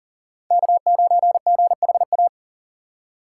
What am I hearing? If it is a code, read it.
Morse code, heard as a keyed tone, about 40 wpm, 700 Hz. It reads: K9GHA
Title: USA Call Signs - 5 Characters - 3x Character Spacing - 40wpm